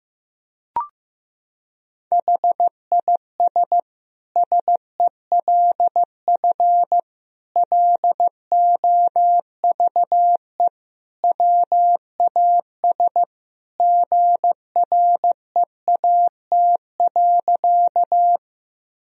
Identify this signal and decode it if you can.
Morse code, heard as a keyed tone, about 15 wpm, 700 Hz. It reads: HIS SELF LOVE WAS GREAT.
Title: Sentences from Top 1000 Words - Review Part 1 - 15wpm